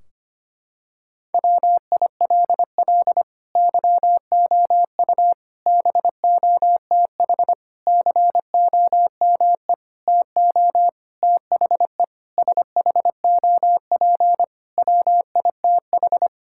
Morse code, heard as a keyed tone, about 25 wpm, 700 Hz. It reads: WILL YOU BOTH COME TO THE SHOP WITH
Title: Sentences from Top 1600 Words 25wpm